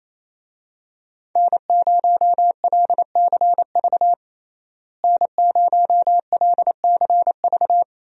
Morse code, heard as a keyed tone, about 28 wpm, 700 Hz. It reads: N0LCV N0LCV
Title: USA Call Signs - 5 Characters - Repeated 3x - 28wpm